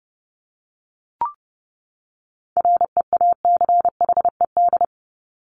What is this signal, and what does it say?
Morse code, heard as a keyed tone, about 30 wpm, 700 Hz. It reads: REACHED
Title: ED Suffix - Top 100 Words with Base Words 30wpm